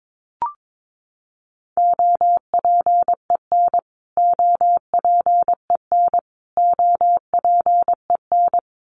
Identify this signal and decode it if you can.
Morse code, heard as a keyed tone, about 22 wpm, 700 Hz. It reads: OPEN OPEN OPEN
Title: Top 500 Words Repeated 22wpm